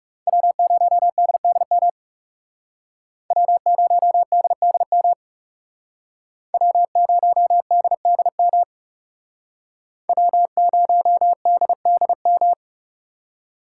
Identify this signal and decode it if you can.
Morse code, heard as a keyed tone, about 39 wpm, 700 Hz. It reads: W0DDM W0DDM W0DDM W0DDM